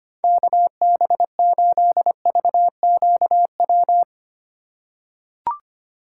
Morse code, heard as a keyed tone, about 25 wpm, 700 Hz. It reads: KB8VQW
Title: USA Call Signs - 6 Characters - 2x Character Spacing - 25wpm